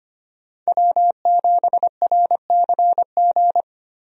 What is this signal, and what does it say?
Morse code, heard as a keyed tone, about 25 wpm, 700 Hz. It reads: W7RCG
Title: USA Call Signs - 5 Characters - Repeated 2x - 25wpm